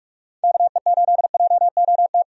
Morse code, heard as a keyed tone, about 46 wpm, 700 Hz. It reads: KE8JOT
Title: USA Call Signs - 6 Characters - Repeated 3x - 45wpm